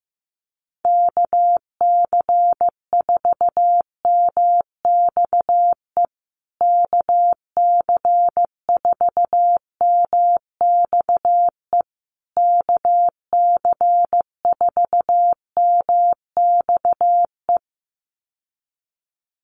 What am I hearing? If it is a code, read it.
Morse code, heard as a keyed tone, about 15 wpm, 700 Hz. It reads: KC4MXE KC4MXE KC4MXE
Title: USA Call Signs Repeated 3x 15wpm